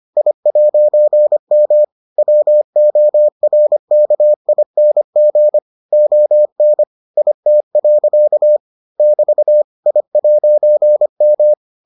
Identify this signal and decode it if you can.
Morse code, heard as a keyed tone, about 25 wpm, 600 Hz. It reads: I'M WORKING ON IT. <BT> I'M